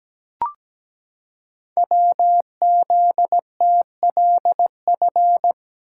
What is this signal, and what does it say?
Morse code, heard as a keyed tone, about 17 wpm, 700 Hz. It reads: WZTLF
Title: Code Groups - 5 Letters - 17wpm